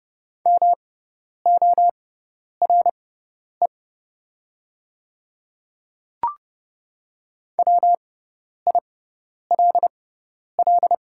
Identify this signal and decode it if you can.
Morse code, heard as a keyed tone, about 30 wpm, spaced as for 10 wpm, 700 Hz. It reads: MORE WILL
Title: 6x Character Spacing - Top 100 Words - 30wpm